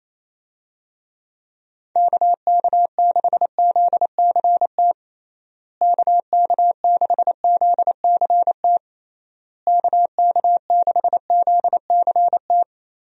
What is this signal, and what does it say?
Morse code, heard as a keyed tone, about 28 wpm, 700 Hz. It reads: KK6ZCT KK6ZCT KK6ZCT
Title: USA Call Signs - 6 Characters - Repeated 3x - 28wpm